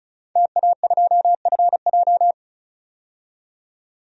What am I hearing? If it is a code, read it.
Morse code, heard as a keyed tone, about 34 wpm, 700 Hz. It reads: TA2FJ
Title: DXCC-100 Call Signs - 5 Characters - 35wpm